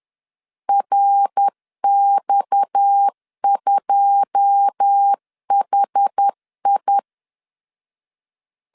Morse code, heard as a keyed tone, about 10 wpm, 800 Hz. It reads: RX2HI
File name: LEZIONI / 9 2 10wpm 2736